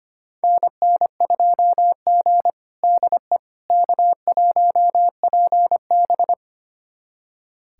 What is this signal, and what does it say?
Morse code, heard as a keyed tone, about 25 wpm, 700 Hz. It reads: NN2G DE K1PB
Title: USA Callsign DE Callsign - 4 Characters - 25wpm